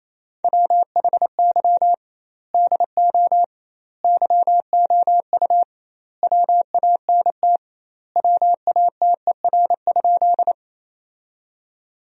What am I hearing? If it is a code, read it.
Morse code, heard as a keyed tone, about 28 wpm, 700 Hz. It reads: WHY DO YOU WANT WATER?
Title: 2x Word-Spacing - Sentences from Top 200 Words - 28wpm